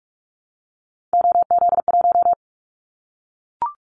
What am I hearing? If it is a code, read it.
Morse code, heard as a keyed tone, about 44 wpm, 700 Hz. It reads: OZ1
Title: Callsign Prefixes for Top 100 DXCC - 45wpm